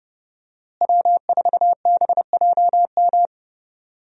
Morse code, heard as a keyed tone, about 32 wpm, 700 Hz. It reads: W4BJM
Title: USA Call Signs - 5 Characters - Repeated 3x - 30wpm